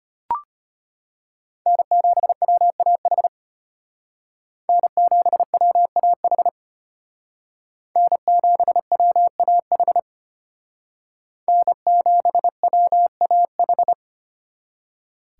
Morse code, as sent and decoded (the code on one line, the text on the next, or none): -. --... .-- .- .... / -. --... .-- .- .... / -. --... .-- .- .... / -. --... .-- .- ....
N7WAH N7WAH N7WAH N7WAH